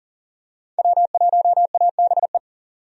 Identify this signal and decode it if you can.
Morse code, heard as a keyed tone, about 40 wpm, 700 Hz. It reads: W1ABE
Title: USA Call Signs - 5 Characters - 40wpm